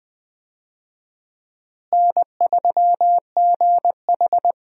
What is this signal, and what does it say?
Morse code, heard as a keyed tone, about 20 wpm, 700 Hz. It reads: N3GH
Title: USA Call Signs - 4 Characters - Repeated 3x - 20wpm